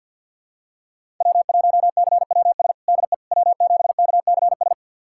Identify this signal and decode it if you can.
Morse code, heard as a keyed tone, about 50 wpm, 700 Hz. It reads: W1CWS DE W7KCS
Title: USA Callsign DE Callsign - 5 Characters - 50wpm